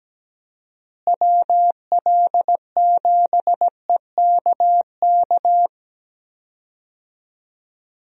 Morse code, heard as a keyed tone, about 17 wpm, 700 Hz. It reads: WL7EKK
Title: USA Call Signs - 6 Characters - Repeated 3x - 17wpm